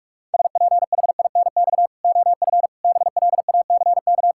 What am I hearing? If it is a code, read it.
Morse code, heard as a keyed tone, about 46 wpm, 700 Hz. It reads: SPHINX OF BLACK
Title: Pangram Challenge v2 - 45wpm